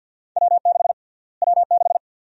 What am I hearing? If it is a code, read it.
Morse code, heard as a keyed tone, about 49 wpm, 700 Hz. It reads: W6 W6
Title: USA Call Sign Prefixes - Repeated 2x - 50wpm